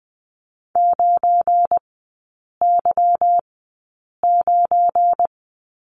Morse code, heard as a keyed tone, about 20 wpm, spaced as for 8 wpm, 690 Hz. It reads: 9Y9
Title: 172 Lesson 28e - Number 9 - Three Characters 20wpm